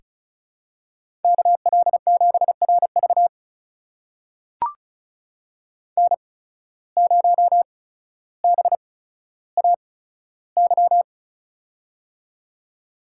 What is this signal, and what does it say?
Morse code, heard as a keyed tone, about 36 wpm, spaced as for 15 wpm, 700 Hz. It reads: KL7RV N0BAY
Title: USA Call Signs - 5 Characters - 8x Character Spacing - 35wpm